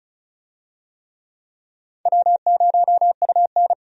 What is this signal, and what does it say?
Morse code, heard as a keyed tone, about 34 wpm, 700 Hz. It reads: W0UN
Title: USA Call Signs - 4 Characters - 35wpm